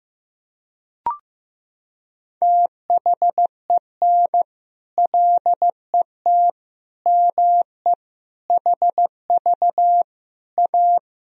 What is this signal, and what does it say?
Morse code, heard as a keyed tone, about 16 wpm, 700 Hz. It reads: THEN LET ME HV A
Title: Sentences from Top 300 Words and Abbreviations 15wpm